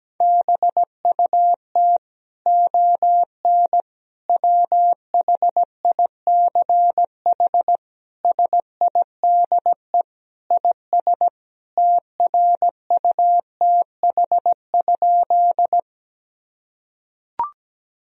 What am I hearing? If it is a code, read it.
Morse code, heard as a keyed tone, about 17 wpm, 700 Hz. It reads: BUT ON WHICH SIDE IS TRUTH?
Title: Sentences from Top 1000 Words - Review Part 1 - 17wpm